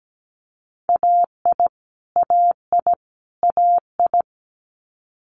Morse code, heard as a keyed tone, about 17 wpm, 700 Hz. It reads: AI AI AI
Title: Multi-letter Phonemes Repeated 3x 17wpm